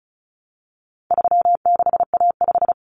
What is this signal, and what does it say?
Morse code, heard as a keyed tone, about 34 wpm, 700 Hz. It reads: 36A5
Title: Common Cut Numbers - 4-Digit 35wpm